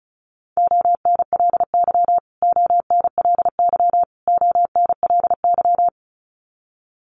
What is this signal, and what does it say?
Morse code, heard as a keyed tone, about 34 wpm, 700 Hz. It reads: ONLY ONLY ONLY
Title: Top 200 Words Repeated 35wpm